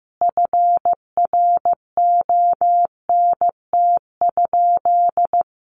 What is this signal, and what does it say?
Morse code, heard as a keyed tone, about 15 wpm, 700 Hz. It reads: FRONT?
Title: Sentences from Top 1400 Words 15wpm